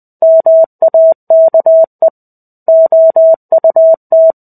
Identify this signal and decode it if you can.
Morse code, heard as a keyed tone, about 20 wpm, 650 Hz. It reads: MAKE OUT
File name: 05-Chapter-05 20 WPM